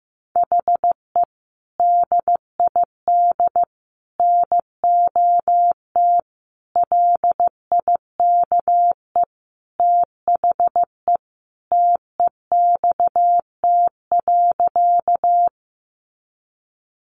Morse code, heard as a keyed tone, about 14 wpm, 700 Hz. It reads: HE DID NOT LIKE THE TEXT.
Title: Sentences from Top 1200 Words 15wpm